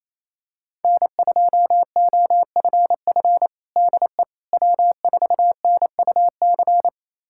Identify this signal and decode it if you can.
Morse code, heard as a keyed tone, about 28 wpm, 700 Hz. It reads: N2OFF DE W4NUC
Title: USA Callsign DE Callsign - 5 Characters - 28wpm